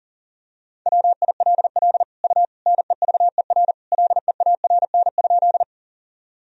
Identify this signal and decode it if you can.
Morse code, heard as a keyed tone, about 40 wpm, 700 Hz. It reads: WILL U NEVER LEARN?
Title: Sentences from Top 300 Words and Abbreviations 40wpm